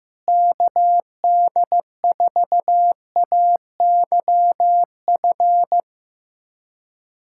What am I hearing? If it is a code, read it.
Morse code, heard as a keyed tone, about 15 wpm, 700 Hz. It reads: KD4AYF
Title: USA Callsign DE Callsign - 6 Characters - 15wpm